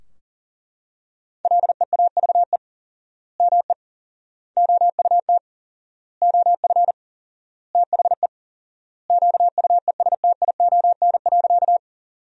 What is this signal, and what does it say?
Morse code, heard as a keyed tone, about 40 wpm, 700 Hz. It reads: LEAVE ME OUT OF THE QUESTION.
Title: 4x Word-Spacing - Sentences from Top 400 Words - 40wpm